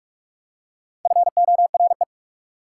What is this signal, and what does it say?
Morse code, heard as a keyed tone, about 44 wpm, 690 Hz. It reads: FORE
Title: Prefixes and Suffixes 45wpm